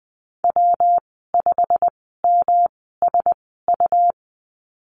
Morse code, heard as a keyed tone, about 20 wpm, spaced as for 15 wpm, 700 Hz. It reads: W5MSU
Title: 122 Lesson 20g - Number 3 - Call Signs 20wpm